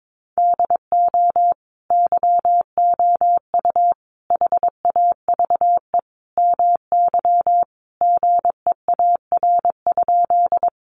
Morse code, heard as a keyed tone, about 22 wpm, 700 Hz. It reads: DO YOU HAVE MY GEAR?
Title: Sentences from Top 1400 Words 22wpm